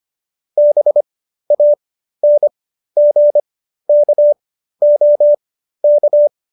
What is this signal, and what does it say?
Morse code, heard as a keyed tone, about 25 wpm, spaced as for 13 wpm, 590 Hz. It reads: BANGKOK
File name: qso208b 13